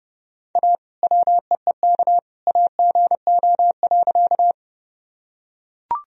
Morse code, heard as a keyed tone, about 30 wpm, 700 Hz. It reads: A WEEK AGO.
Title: Sentences from Top 1000 Words - Review Part 3 - 30wpm